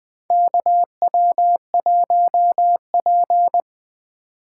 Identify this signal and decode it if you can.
Morse code, heard as a keyed tone, about 20 wpm, 700 Hz. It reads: KW1P
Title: USA Call Signs - 2x Character Spacing - 20wpm